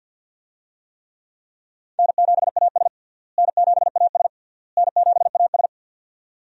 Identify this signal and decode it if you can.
Morse code, heard as a keyed tone, about 52 wpm, 700 Hz. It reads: N7AS N7AS N7AS
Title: USA Call Signs - 4 Characters - Repeated 3x - 50wpm